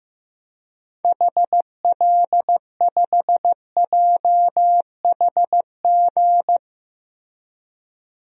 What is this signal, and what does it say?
Morse code, heard as a keyed tone, about 15 wpm, 700 Hz. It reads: HL5JHG
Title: DXCC-100 Call Signs - 6 Characters - 15wpm